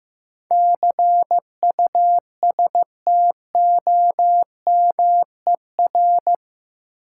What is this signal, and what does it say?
Morse code, heard as a keyed tone, about 15 wpm, 700 Hz. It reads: CUSTOMER
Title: Sentences from Top 1100 Words - New Word Drill - 15wpm